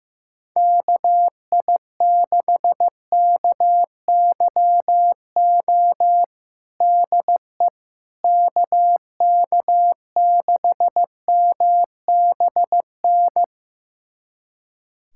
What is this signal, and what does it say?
Morse code, heard as a keyed tone, about 15 wpm, 700 Hz. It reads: KI6KYO DE KK6MBN
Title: USA Callsign DE Callsign - 6 Characters - 15wpm